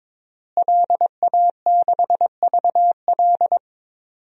Morse code, heard as a keyed tone, about 22 wpm, 700 Hz. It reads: LA6VL
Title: DXCC-100 Call Signs - 5 Characters - Repeated 3x - 22wpm